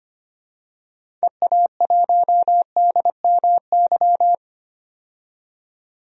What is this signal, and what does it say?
Morse code, heard as a keyed tone, about 25 wpm, 700 Hz. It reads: EA1DMY
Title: DXCC-100 Call Signs - 6 Characters - 25wpm